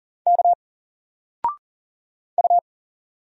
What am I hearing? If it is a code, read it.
Morse code, heard as a keyed tone, about 40 wpm, 700 Hz. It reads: K U
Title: Single Letter-Number 40wpm